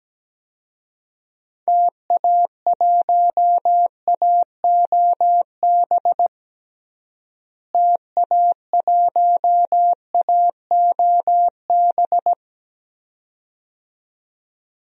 Morse code, heard as a keyed tone, about 17 wpm, 700 Hz. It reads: TA1AOB TA1AOB
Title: DXCC-100 Call Signs - 6 Characters - Repeated 2x - 17wpm